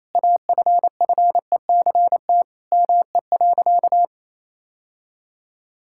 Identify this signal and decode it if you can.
Morse code, heard as a keyed tone, about 28 wpm, 700 Hz. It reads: AFFECT ME.
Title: Sentences from Top 1000 Words - Review Part 6 - 28wpm